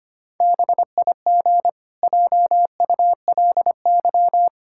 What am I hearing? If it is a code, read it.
Morse code, heard as a keyed tone, about 25 wpm, 700 Hz. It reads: BIG JULY
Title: Pangram Challenge v2 - 25wpm